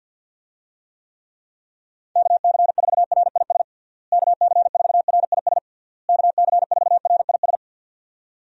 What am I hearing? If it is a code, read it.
Morse code, heard as a keyed tone, about 47 wpm, 700 Hz. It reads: KC4RIS KC4RIS KC4RIS
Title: USA Call Signs - 6 Characters - Repeated 3x - 50wpm